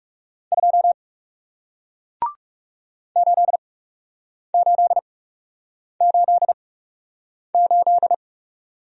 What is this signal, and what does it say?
Morse code, heard as a keyed tone, about 41 wpm, spaced as for 8 wpm, 700 Hz. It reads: 2 8888